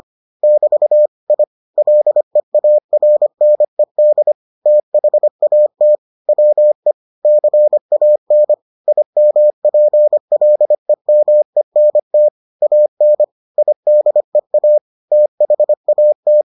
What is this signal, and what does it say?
Morse code, heard as a keyed tone, about 25 wpm, 600 Hz. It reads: <BT> I LEARNED THAT WE CAN IMPLEMENT AN IDEA THAT